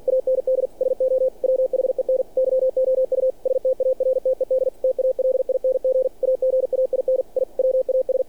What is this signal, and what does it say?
Morse code, heard as a keyed tone, about 48 wpm, 520 Hz. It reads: AND SO WHEN YOU STARTED TALING AGAIN I WAS
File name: archived / CRSnet